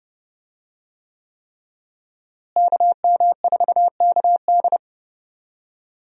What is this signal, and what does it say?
Morse code, heard as a keyed tone, about 32 wpm, 700 Hz. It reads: KM4KD
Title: USA Call Signs - 5 Characters - Repeated 3x - 30wpm